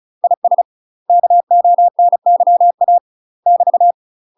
Morse code, heard as a keyed tone, about 34 wpm, 700 Hz. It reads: IS KONYA <BT>
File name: UK MASTER QSOs TRAINING / 35WPM / UK MASTER QSOs 0037